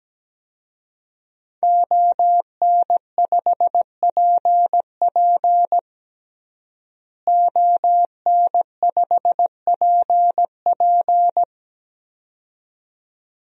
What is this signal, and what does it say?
Morse code, heard as a keyed tone, about 17 wpm, 700 Hz. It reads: ON5PP ON5PP
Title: DXCC-100 Call Signs - 5 Characters - Repeated 2x - 17wpm